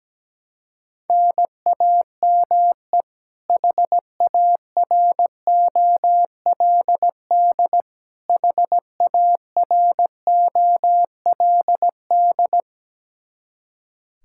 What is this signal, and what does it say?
Morse code, heard as a keyed tone, about 17 wpm, 700 Hz. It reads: NAME HAROLD HAROLD
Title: Practice QSO - Exchange 1 - Beginning - 17wpm